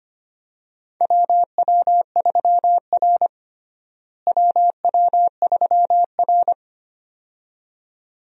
Morse code, heard as a keyed tone, about 25 wpm, 700 Hz. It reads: WW3R WW3R
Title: USA Call Signs - 4 Characters - Repeated 2x - 25wpm